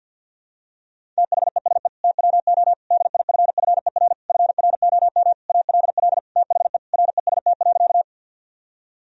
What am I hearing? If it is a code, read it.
Morse code, heard as a keyed tone, about 52 wpm, 700 Hz. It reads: THESE TWO DIFFER FROM ALL THE REST.
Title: Sentences from Top 1000 Words - Review Part 1 - 50wpm